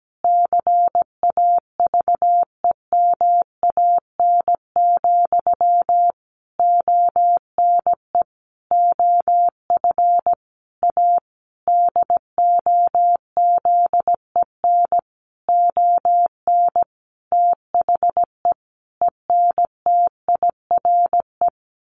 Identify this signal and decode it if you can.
Morse code, heard as a keyed tone, about 17 wpm, 700 Hz. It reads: CAVEMAN, ONE OF A DOZEN ON THE ENTIRE
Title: A Sound Of Thunder 17wpm